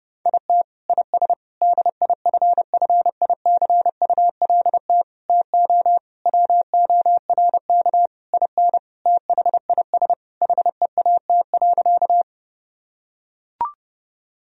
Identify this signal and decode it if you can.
Morse code, heard as a keyed tone, about 32 wpm, 700 Hz. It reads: IT IS DIFFICULT TO WORK IN THIS HEAT.
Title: Sentences from Top 1000 Words - Review Part 3 - 30wpm